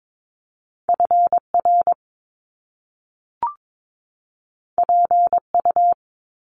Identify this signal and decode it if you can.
Morse code, heard as a keyed tone, about 22 wpm, 700 Hz. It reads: FR PU
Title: Top Two Letter Combinations in Words 22wpm